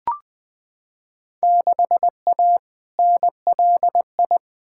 Morse code, heard as a keyed tone, about 20 wpm, 700 Hz. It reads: 6A NLI
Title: Field Day Exchanges Encore - 20wpm